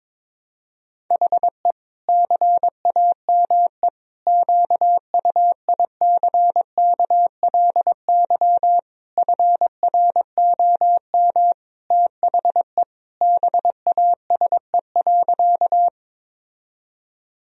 Morse code, heard as a keyed tone, about 22 wpm, 700 Hz. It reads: HE CAME QUICKLY FROM THE BASE.
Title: Sentences from Top 1000 Words - Review Part 1 - 22wpm